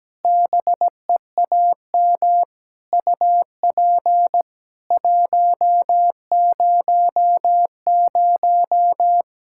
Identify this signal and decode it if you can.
Morse code, heard as a keyed tone, about 17 wpm, 700 Hz. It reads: BEAM UP 100
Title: K1USN SST QSO Practice 17wpm